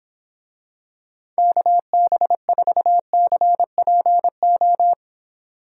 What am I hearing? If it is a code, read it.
Morse code, heard as a keyed tone, about 26 wpm, 700 Hz. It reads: KB4CPO